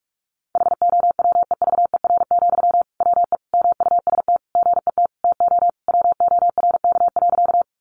Morse code, heard as a keyed tone, about 44 wpm, 700 Hz. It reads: HOWEVER, WE MUST GET TO WORK.
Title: Sentences from Top 600 Words 45wpm